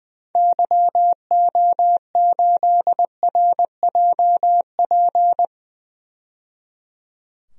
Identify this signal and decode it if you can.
Morse code, heard as a keyed tone, about 20 wpm, 700 Hz. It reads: YO8RJP
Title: DXCC-100 Call Signs - 6 Characters - 20wpm